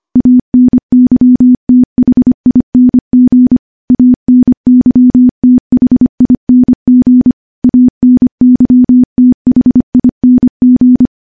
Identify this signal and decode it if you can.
Morse code, heard as a keyed tone, about 25 wpm, 250 Hz. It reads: ANYTHING ANYTHING ANYTHING
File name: morse / morse-training-25WPM